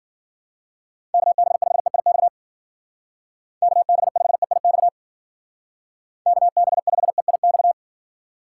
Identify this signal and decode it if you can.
Morse code, heard as a keyed tone, about 49 wpm, 700 Hz. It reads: KB5EIX KB5EIX KB5EIX
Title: Speed Racing - USA 2x3 Call Signs 40wpm